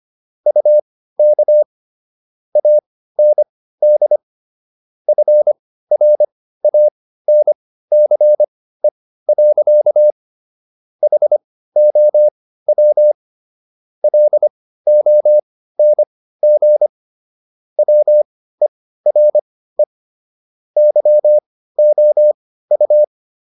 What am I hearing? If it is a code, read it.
Morse code, heard as a keyed tone, about 25 wpm, spaced as for 15 wpm, 600 Hz. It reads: UK AND FRANCE. HOW LONG WERE YOU